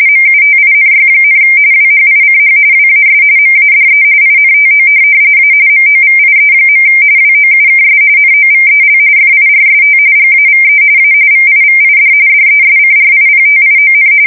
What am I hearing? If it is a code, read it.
RTTY, about 45.45 baud. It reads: MUTINY, WHICH WAS MADE INTO A MOVIE. WOUKS NOVEL FIRST GAVE A VIVID DESCRIPTION
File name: fsk170 / USS-New-Jersey